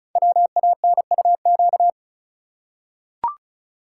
Morse code, heard as a keyed tone, about 36 wpm, 700 Hz. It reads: WANUQ
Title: Code Groups - 5 Letter:Numbers - 35wpm